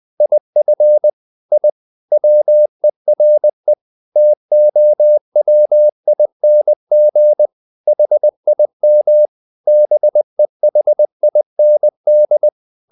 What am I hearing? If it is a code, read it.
Morse code, heard as a keyed tone, about 20 wpm, 600 Hz. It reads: IF I WERE TOWING HIM BEHIND
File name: the old man and the sea part 2 20wpm